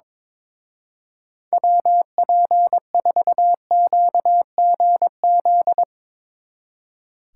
Morse code, heard as a keyed tone, about 22 wpm, 700 Hz. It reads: WP4QGZ
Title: Speed Racing - USA Call Signs 22wpm